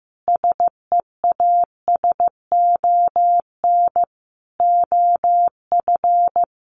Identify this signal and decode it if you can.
Morse code, heard as a keyed tone, about 15 wpm, 700 Hz. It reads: SEASON OF
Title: Sentences from Top 1400 Words 15wpm